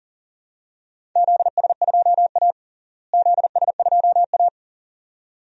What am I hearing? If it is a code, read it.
Morse code, heard as a keyed tone, about 40 wpm, 700 Hz. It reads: ZS2A ZS2A
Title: DXCC-100 Call Signs - 4 Characters - Repeated 2x - 40wpm